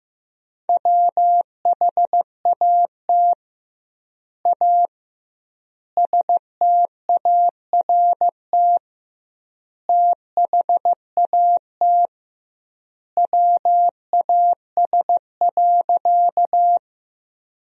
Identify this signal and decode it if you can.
Morse code, heard as a keyed tone, about 16 wpm, 700 Hz. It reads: WHAT A START THAT WAS.
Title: 2x Word-Spacing - Sentences from Top 300 Words - 15wpm